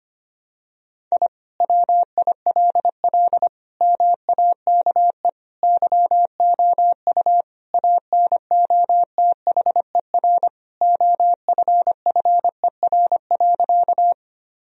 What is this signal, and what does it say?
Morse code, heard as a keyed tone, about 25 wpm, 700 Hz. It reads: I WILL MAKE YOU ANOTHER OFFER.
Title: Sentences from Top 1000 Words - Review Part 4 - 25wpm